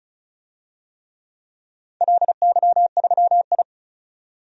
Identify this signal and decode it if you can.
Morse code, heard as a keyed tone, about 36 wpm, 700 Hz. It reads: LY3I
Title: DXCC-100 Call Signs - 4 Characters - 35wpm